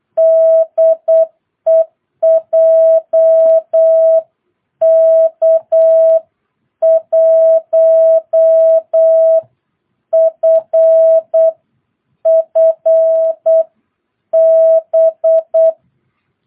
Morse code, heard as a keyed tone, about 8 wpm, spaced as for 7 wpm, 650 Hz. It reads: DE JK1FFB
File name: jh8cbh / 2022cw